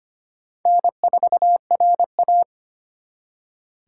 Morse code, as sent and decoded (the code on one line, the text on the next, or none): -. ....- .-. .-
N4RA